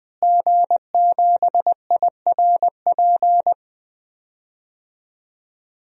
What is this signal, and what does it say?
Morse code, heard as a keyed tone, about 20 wpm, 700 Hz. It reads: G7IRP
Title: DXCC-100 Call Signs - 5 Characters - Repeated 3x - 20wpm